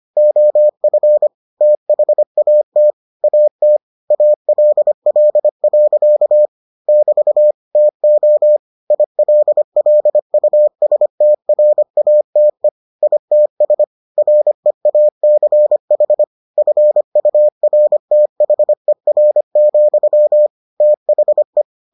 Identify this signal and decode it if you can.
Morse code, heard as a keyed tone, about 25 wpm, 600 Hz. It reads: OF THAT AT ALL. <BT> TO ILLUSTRATE ITS REACH FURTHER, THE